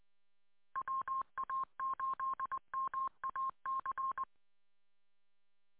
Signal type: Morse code